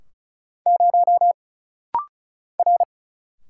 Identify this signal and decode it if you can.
Morse code, heard as a keyed tone, about 35 wpm, 700 Hz. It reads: 0 R
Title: Single Letter-Number - Rapid-Fire 35wpm